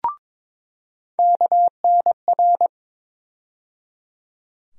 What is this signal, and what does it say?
Morse code, heard as a keyed tone, about 22 wpm, 700 Hz. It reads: KNR
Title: USA Call Sign Postfixes - 3 Letters 22wpm